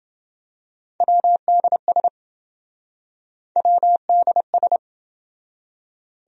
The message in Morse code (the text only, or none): WDS WDS